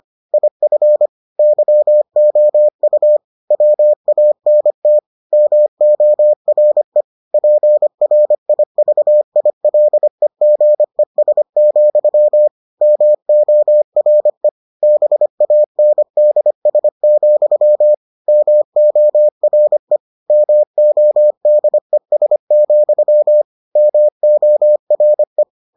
Morse code, heard as a keyed tone, about 25 wpm, 600 Hz. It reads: IF YOU WANT MORE PRIVILEGES, MORE BANDS, MORE MODES, MORE